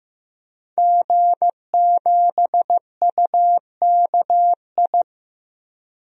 Morse code, heard as a keyed tone, about 15 wpm, 700 Hz. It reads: G7UKI
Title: DXCC-100 Call Signs - 5 Characters - Repeated 3x - 15wpm